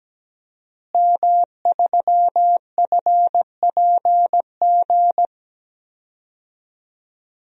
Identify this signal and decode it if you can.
Morse code, heard as a keyed tone, about 17 wpm, 700 Hz. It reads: M3FPG
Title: DXCC-100 Call Signs - 5 Characters - Repeated 3x - 17wpm